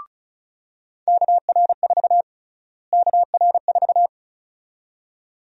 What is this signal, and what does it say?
Morse code, heard as a keyed tone, about 36 wpm, 700 Hz. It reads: KR4 KR4
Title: USA Call Sign Prefixes - Repeated 2x - 35wpm